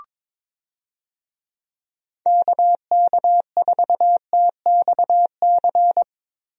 Morse code, heard as a keyed tone, about 22 wpm, 700 Hz. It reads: KK4TXC
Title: USA Call Signs - 6 Characters - Repeated 3x - 22wpm